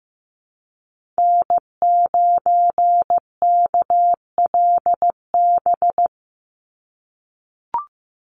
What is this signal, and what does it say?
Morse code, heard as a keyed tone, about 15 wpm, 700 Hz. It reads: N9KLB
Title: USA Call Signs 15wpm